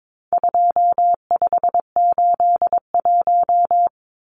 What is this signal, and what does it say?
Morse code, heard as a keyed tone, about 22 wpm, 700 Hz. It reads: 2581
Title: Numbers - 4-Digit 22wpm